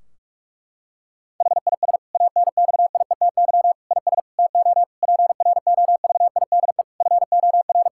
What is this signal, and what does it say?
Morse code, heard as a keyed tone, about 41 wpm, 700 Hz. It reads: HIS ANXIETY IS TO PROVIDE FOR
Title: Sentences from Top 1600 Words 45wpm